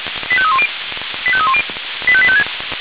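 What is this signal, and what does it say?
Five-tone selective call (ZVEI), heard as quick runs of five tones, tones of about 70 ms. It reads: 85310 85310 85856
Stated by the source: ZVEI five-tone selective call received with an SDR, from the audio server of Feuerwehr Orlamünde.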